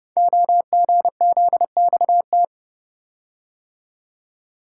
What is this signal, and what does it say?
Morse code, heard as a keyed tone, about 30 wpm, 700 Hz. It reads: OGZXT
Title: Code Groups - 5 Letters - 30wpm